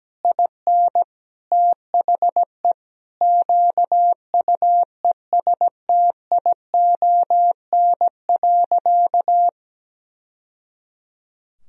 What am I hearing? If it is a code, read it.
Morse code, heard as a keyed tone, about 17 wpm, 700 Hz. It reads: IN THE QUESTION.
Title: Sentences from Top 1000 Words - Review Part 2 - 17wpm